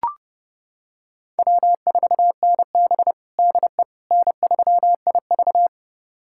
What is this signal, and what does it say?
Morse code, heard as a keyed tone, about 32 wpm, 700 Hz. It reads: W4NB DE N3IV
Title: USA Callsign DE Callsign - 4 Characters - 30wpm